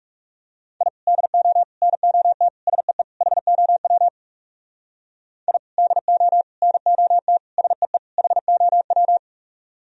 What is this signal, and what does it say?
Morse code, heard as a keyed tone, about 40 wpm, 700 Hz. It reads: I DO NOT SEE HOW I DO NOT SEE HOW